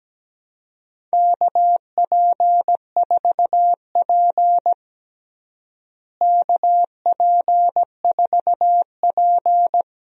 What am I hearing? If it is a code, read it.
Morse code, heard as a keyed tone, about 17 wpm, 700 Hz. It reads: KP4P KP4P
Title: DXCC-100 Call Signs - 4 Characters - Repeated 3x - 17wpm